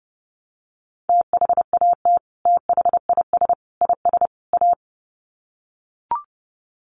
Morse code, heard as a keyed tone, about 30 wpm, 700 Hz. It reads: THAT THIS IS A
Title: Sets of 4 Words 30wpm